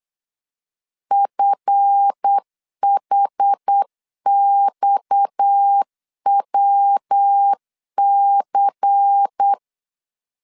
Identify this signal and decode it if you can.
Morse code, heard as a keyed tone, about 8 wpm, 800 Hz. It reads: FHXWC
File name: LEZIONI / 10 SIGNS / 10A SLASH 8wpm 3452